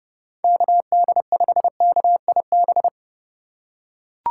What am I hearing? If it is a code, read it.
Morse code, heard as a keyed tone, about 32 wpm, 700 Hz. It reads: KD5KIB
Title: USA Call Signs - 6 Characters - 3x Character Spacing - 30wpm